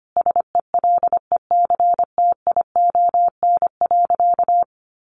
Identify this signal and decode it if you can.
Morse code, heard as a keyed tone, about 25 wpm, 700 Hz. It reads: SELECTION.
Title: Sentences from Top 1400 Words 25wpm